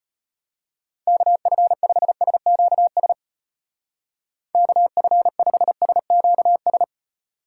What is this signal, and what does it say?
Morse code, heard as a keyed tone, about 33 wpm, 700 Hz. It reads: KF5SQS KF5SQS